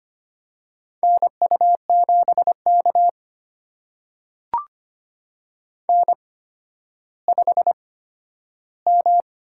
Morse code, heard as a keyed tone, about 25 wpm, spaced as for 12 wpm, 700 Hz. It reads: NU7K N5M
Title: USA Call Signs - 8x Character Spacing - 25wpm